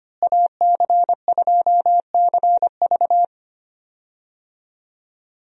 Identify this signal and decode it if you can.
Morse code, heard as a keyed tone, about 25 wpm, 700 Hz. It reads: AC2CV
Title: USA Call Signs - 5 Characters - Repeated 3x - 25wpm